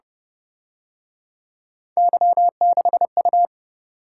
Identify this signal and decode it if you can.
Morse code, heard as a keyed tone, about 31 wpm, 700 Hz. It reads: Y6U